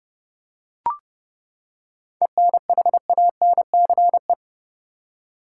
Morse code, heard as a keyed tone, about 30 wpm, 700 Hz. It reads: ENHANCE
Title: Seven Letter Words - 30wpm